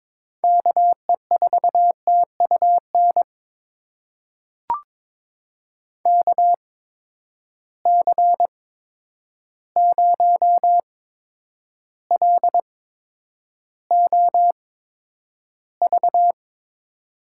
Morse code, heard as a keyed tone, about 21 wpm, spaced as for 9 wpm, 700 Hz. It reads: KE4TUN KC0LOV
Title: USA Call Signs - 6 Characters - 8x Character Spacing - 22wpm